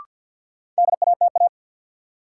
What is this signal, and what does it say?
Morse code, heard as a keyed tone, about 53 wpm, 700 Hz. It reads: DATA
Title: Sentences from Top 1000 Words - Review Part 3 - New Word Drill - 50wpm